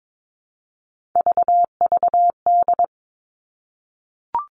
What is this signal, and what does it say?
Morse code, heard as a keyed tone, about 22 wpm, 700 Hz. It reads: VVD
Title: Three Challenging Letter-Number Combinations 22wpm